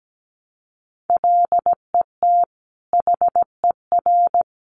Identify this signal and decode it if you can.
Morse code, heard as a keyed tone, about 17 wpm, 700 Hz. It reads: LET HER
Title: Sentences from Top 300 Words 17wpm